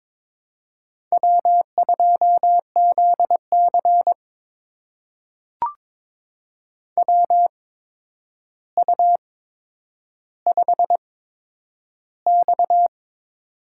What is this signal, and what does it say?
Morse code, heard as a keyed tone, about 23 wpm, spaced as for 10 wpm, 700 Hz. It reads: W2ZC WU5X
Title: USA Call Signs - 4 Characters - 8x Character Spacing - 22wpm